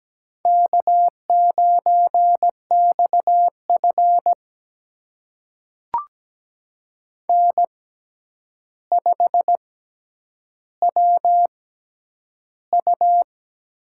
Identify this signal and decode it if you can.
Morse code, heard as a keyed tone, about 17 wpm, spaced as for 9 wpm, 700 Hz. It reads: K9XF N5WU
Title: USA Call Signs - 4 Characters - 6x Character Spacing - 17wpm